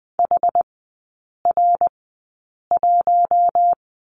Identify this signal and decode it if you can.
Morse code, heard as a keyed tone, about 20 wpm, spaced as for 8 wpm, 700 Hz. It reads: HR1
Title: 067 Lesson 12e - Letter d - Three Characters 20wpm